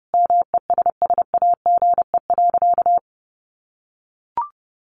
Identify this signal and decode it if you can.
Morse code, heard as a keyed tone, about 30 wpm, 700 Hz. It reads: MESSAGE.
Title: Sentences from Top 1100 Words 30wpm